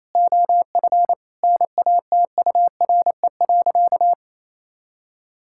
Morse code, heard as a keyed tone, about 28 wpm, 700 Hz. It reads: OF NATURE.
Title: Sentences from Top 1000 Words - Review Part 5 - 28wpm